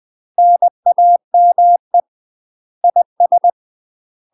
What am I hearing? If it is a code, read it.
Morse code, heard as a keyed tone, about 20 wpm, 700 Hz. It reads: NAME IS
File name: UK MASTER QSOs TRAINING / 20WPM / UK MASTER QSOs 0022